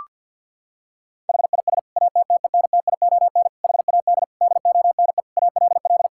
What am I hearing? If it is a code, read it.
Morse code, heard as a keyed tone, about 50 wpm, 700 Hz. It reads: HIS ATTENTION HAD DONE ALL
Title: Sentences from Top 1000 Words - Review Part 6 - 50wpm